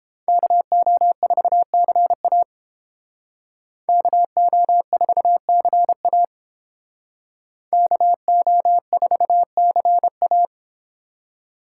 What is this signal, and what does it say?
Morse code, heard as a keyed tone, about 30 wpm, 700 Hz. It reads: KO4CA KO4CA KO4CA